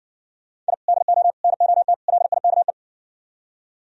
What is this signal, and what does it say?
Morse code, heard as a keyed tone, about 62 wpm, 700 Hz. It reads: I DO NOT LIKE